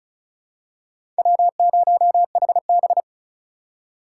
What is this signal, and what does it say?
Morse code, heard as a keyed tone, about 34 wpm, 700 Hz. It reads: W0HB
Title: USA Call Signs - 4 Characters - 4x Character Spacing - 35wpm